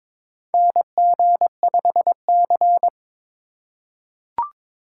Morse code, heard as a keyed tone, about 21 wpm, 700 Hz. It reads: NG5C
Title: USA Call Signs - 4 Characters - 4x Character Spacing - 22wpm